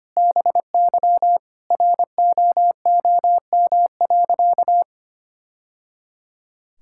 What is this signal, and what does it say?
Morse code, heard as a keyed tone, about 25 wpm, 700 Hz. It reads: BY ROOM.
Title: Sentences from Top 1000 Words - Review Part 5 - 25wpm